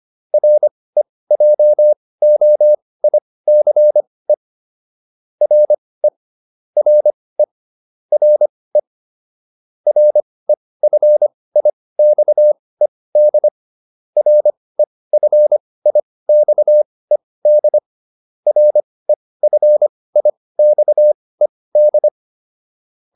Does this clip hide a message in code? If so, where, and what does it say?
Yes: Morse code from 0.3 to 22.1 s, heard as a keyed tone, about 25 wpm, spaced as for 18 wpm, 600 Hz. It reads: REJOICE RE RE RE REFIXED REFIXED REFIXED